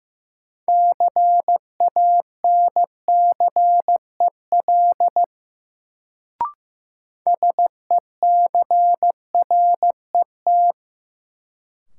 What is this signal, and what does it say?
Morse code, heard as a keyed tone, about 15 wpm, 700 Hz. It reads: CANCEL SECRET
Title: Sentences from Top 1200 Words - New Word Drill - 15wpm